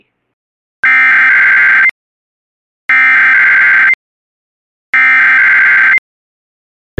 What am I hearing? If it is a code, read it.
Emergency Alert System (SAME) tones (header bursts), heard as harsh buzzing bursts. It is heard from 0.9 to 5.9 s.